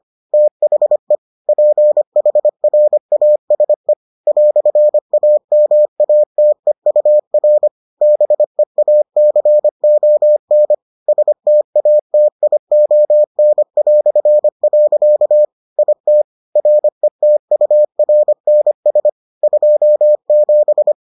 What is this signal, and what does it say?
Morse code, heard as a keyed tone, about 25 wpm, 600 Hz. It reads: THE PHRASE "AMATEUR BEACON STATION". IT RETURNS 27